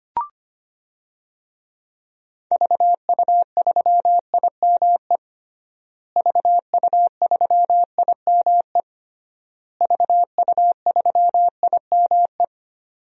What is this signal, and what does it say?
Morse code, heard as a keyed tone, about 25 wpm, 700 Hz. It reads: VU3IME VU3IME VU3IME
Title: DXCC-100 Call Signs - 6 Characters - Repeated 3x - 25wpm